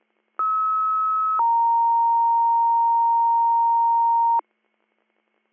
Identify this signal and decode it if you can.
Two-tone page, heard as a one-second tone, then a three-second tone, about 1290 then 930 Hz. It is heard from 0.4 to 4.4 s.